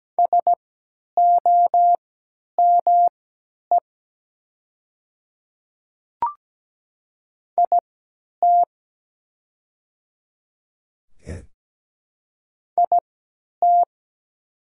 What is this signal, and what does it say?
Morse code, heard as a keyed tone, about 17 wpm, spaced as for 10 wpm, 700 Hz. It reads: SOME IT IT
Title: 3x Character Spacing - Top 100 Words - 17wpm